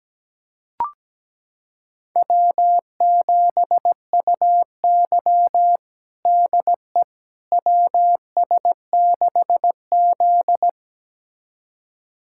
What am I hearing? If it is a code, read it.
Morse code, heard as a keyed tone, about 17 wpm, 700 Hz. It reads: W7UY DE WS6Z
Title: USA Callsign DE Callsign - 4 Characters - 17wpm